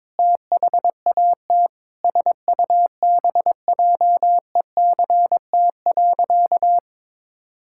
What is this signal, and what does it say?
Morse code, heard as a keyed tone, about 21 wpm, 700 Hz. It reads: THAT SUBJECT.
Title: Sentences from Top 1600 Words 22wpm